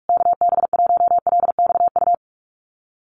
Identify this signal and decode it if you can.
Morse code, heard as a keyed tone, about 44 wpm, 690 Hz. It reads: KB1LXU
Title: USA Call Signs Repeated 3x 45wpm